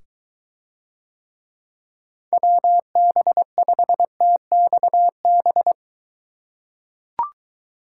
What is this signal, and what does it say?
Morse code, heard as a keyed tone, about 23 wpm, 700 Hz. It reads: WB5TXB